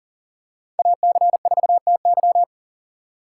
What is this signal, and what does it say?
Morse code, heard as a keyed tone, about 40 wpm, 700 Hz. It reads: AC4TY
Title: USA Call Signs - 5 Characters - 2x Character Spacing - 40wpm